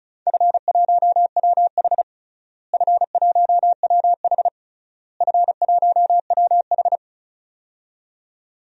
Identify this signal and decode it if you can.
Morse code, heard as a keyed tone, about 36 wpm, 700 Hz. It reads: F1WH F1WH F1WH
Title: DXCC-100 Call Signs - 4 Characters - Repeated 3x - 35wpm